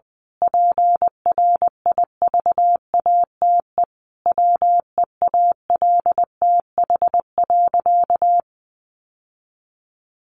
Morse code, heard as a keyed tone, about 20 wpm, 700 Hz. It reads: PRIVATE WEALTH.
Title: Sentences from Top 1200 Words 20wpm